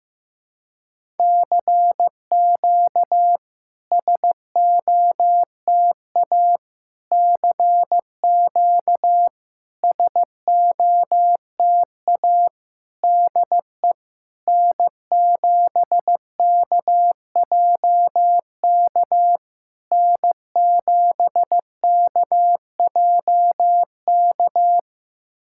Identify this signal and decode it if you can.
Morse code, heard as a keyed tone, about 15 wpm, 690 Hz. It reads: CQ SOTA CQ SOTA DE N7KJK N7KJK
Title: SOTA QSO Practice 15wpm